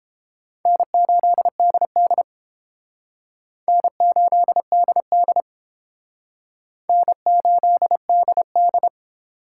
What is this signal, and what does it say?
Morse code, heard as a keyed tone, about 30 wpm, 700 Hz. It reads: N8DD N8DD N8DD